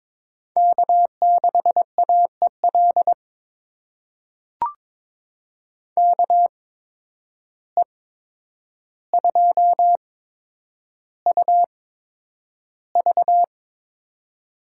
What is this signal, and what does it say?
Morse code, heard as a keyed tone, about 22 wpm, spaced as for 9 wpm, 700 Hz. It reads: K6AEL KE2UV
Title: USA Call Signs - 5 Characters - 8x Character Spacing - 22wpm